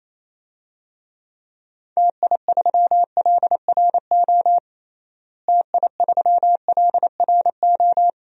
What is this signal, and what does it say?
Morse code, heard as a keyed tone, about 27 wpm, 700 Hz. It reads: TI3LRO TI3LRO
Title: DXCC-100 Call Signs - 6 Characters - Repeated 2x - 28wpm